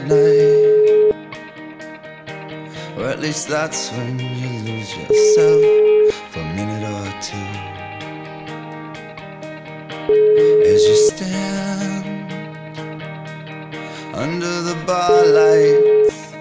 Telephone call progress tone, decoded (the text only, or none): ringback tone